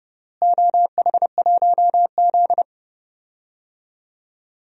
Morse code, heard as a keyed tone, about 30 wpm, 700 Hz. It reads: OH1Z
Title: DXCC-100 Call Signs - 4 Characters - Repeated 2x - 30wpm